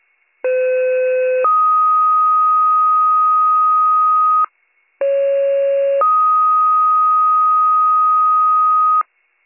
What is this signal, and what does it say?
Two-tone page, heard as a one-second tone, then a three-second tone, about 510 then 1250 Hz, about 550 then 1250 Hz. It is heard from 0.4 to 9.0 s.